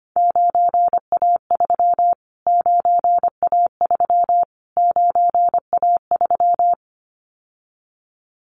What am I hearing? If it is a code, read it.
Morse code, heard as a keyed tone, about 25 wpm, 700 Hz. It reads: 9A3 9A3 9A3
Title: Callsign Prefixes for Top 25 DXCC - Repeated 3x - 25wpm